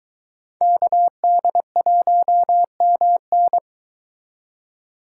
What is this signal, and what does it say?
Morse code, heard as a keyed tone, about 22 wpm, 700 Hz. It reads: KD1MN